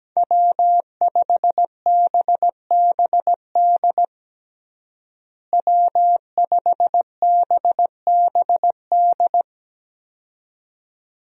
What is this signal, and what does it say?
Morse code, heard as a keyed tone, about 17 wpm, 700 Hz. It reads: W5BBD W5BBD
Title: USA Call Signs - 5 Characters - Repeated 2x - 17wpm